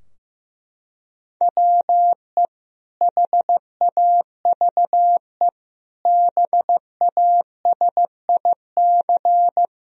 Morse code, heard as a keyed tone, about 15 wpm, 700 Hz. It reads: WE HAVE BASIC
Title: Sentences from Top 1000 Words - Review Part 3 - 15wpm